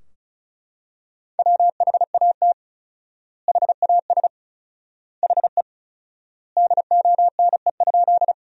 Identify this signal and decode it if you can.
Morse code, heard as a keyed tone, about 36 wpm, 700 Hz. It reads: WHAT HAS HE DONE?
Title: 4x Word-Spacing - Sentences from Top 500 Words - 35wpm